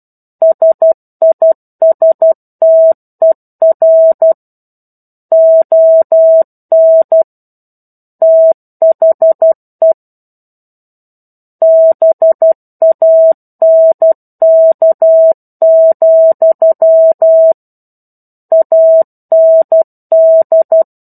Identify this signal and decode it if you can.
Morse code, heard as a keyed tone, about 12 wpm, 650 Hz. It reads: SISTER ON THE BANK, AND
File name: 01-Chapter-01 12 WPM